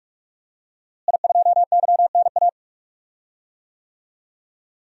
Morse code, heard as a keyed tone, about 46 wpm, 700 Hz. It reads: I2YNA
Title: DXCC-100 Call Signs - 5 Characters - Repeated 2x - 45wpm